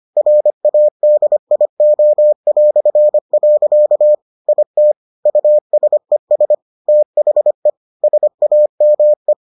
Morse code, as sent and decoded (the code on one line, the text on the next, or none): .-. .- -.. .. --- .-..-. .-.-.- / .. - / ..- ... ... / - .... / ... .- -- .
RADIO". IT USES THE SAME